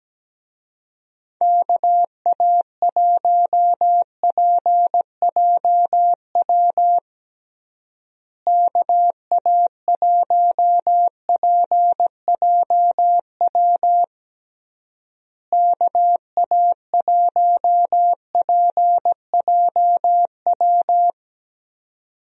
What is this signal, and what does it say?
Morse code, heard as a keyed tone, about 17 wpm, 700 Hz. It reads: KA1PJW KA1PJW KA1PJW
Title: USA Call Signs - 6 Characters - Repeated 3x - 17wpm